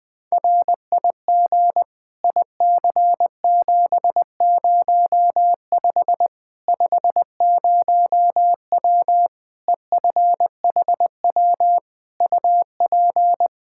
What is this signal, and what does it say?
Morse code, heard as a keyed tone, about 20 wpm, 700 Hz. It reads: RIG IC705 50W EFHW UP
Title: Practice QSO - Exchange 2 - Info - 20wpm